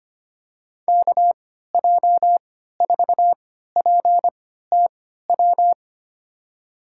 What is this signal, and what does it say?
Morse code, heard as a keyed tone, about 25 wpm, spaced as for 14 wpm, 700 Hz. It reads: KJ4PTW
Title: USA Call Signs - 6 Characters - 3x Character Spacing - 25wpm